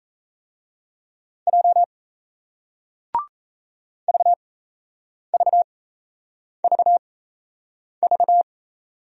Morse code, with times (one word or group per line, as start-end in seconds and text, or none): 1.5-1.8 J
4.1-8.4 VVVV